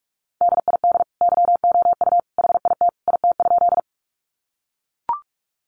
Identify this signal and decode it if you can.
Morse code, heard as a keyed tone, about 44 wpm, 700 Hz. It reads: DID YOU HIT IT?
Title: Sentences from Top 1200 Words 45wpm